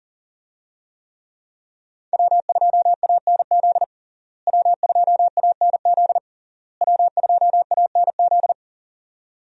Morse code, heard as a keyed tone, about 40 wpm, 700 Hz. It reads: W2ANZ W2ANZ W2ANZ
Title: USA Call Signs - 5 Characters - Repeated 3x - 40wpm